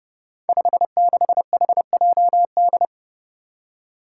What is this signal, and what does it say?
Morse code, heard as a keyed tone, about 30 wpm, 700 Hz. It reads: 56HJD
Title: Five Challenging Letter-Number Combinations 30wpm